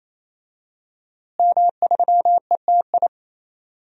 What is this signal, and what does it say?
Morse code, heard as a keyed tone, about 28 wpm, 700 Hz. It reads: M3ETI
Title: DXCC-100 Call Signs - 5 Characters - Repeated 3x - 28wpm